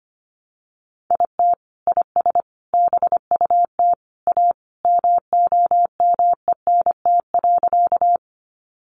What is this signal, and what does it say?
Morse code, heard as a keyed tone, about 25 wpm, 700 Hz. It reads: IT IS BUT A MOMENT.
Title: Sentences from Top 1000 Words 25wpm